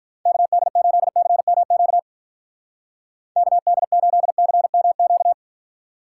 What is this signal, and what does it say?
Morse code, heard as a keyed tone, about 43 wpm, 700 Hz. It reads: KD8CMQ KD8CMQ
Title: Speed Racing - USA 2x3 Call Signs 35wpm